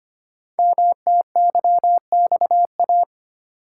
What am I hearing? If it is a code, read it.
Morse code, heard as a keyed tone, about 25 wpm, 700 Hz. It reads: MTYXA
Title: Code Groups - 5 Letters - 25wpm